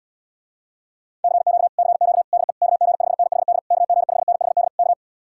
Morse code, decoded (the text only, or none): CQ CQ DE KG5NHD KG5NHD K